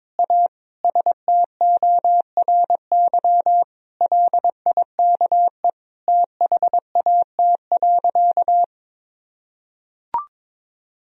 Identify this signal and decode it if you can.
Morse code, heard as a keyed tone, about 22 wpm, 700 Hz. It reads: A STORY LIKE THAT.
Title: Sentences from Top 1000 Words - Review Part 3 - 22wpm